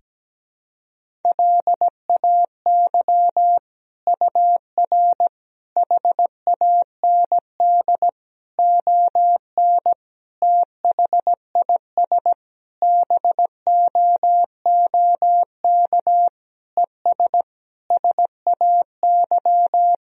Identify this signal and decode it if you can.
Morse code, heard as a keyed tone, about 17 wpm, 700 Hz. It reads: LAY UR HAND ON THIS BOOK ES SAY
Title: Sentences from Top 400 Words and Abbreviations 17wpm